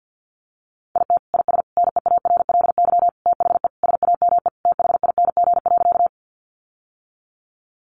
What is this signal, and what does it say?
Morse code, heard as a keyed tone, about 50 wpm, 700 Hz. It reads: IT IS NEARLY THE SAME THING.
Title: Sentences from Top 1300 Words 50wpm